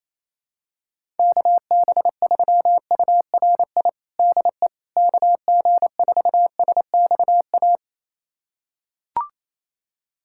Morse code, heard as a keyed tone, about 28 wpm, 700 Hz. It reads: KB3URI DE KG4SXA
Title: USA Callsign DE Callsign 28wpm